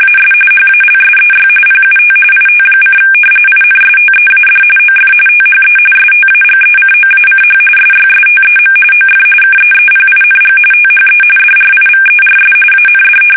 RTTY, about 45.45 baud. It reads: AS HALSEY PUT IT: INSTEAD OF THE STAGECOACH SYSTEM OF KEEPING THE DRIVERS AND
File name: fsk850 / battleshipnj